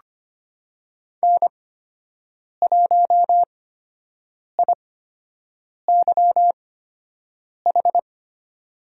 Morse code, heard as a keyed tone, about 25 wpm, spaced as for 7 wpm, 700 Hz. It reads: N1IYH
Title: USA Call Signs - 5 Characters - 8x Character Spacing - 25wpm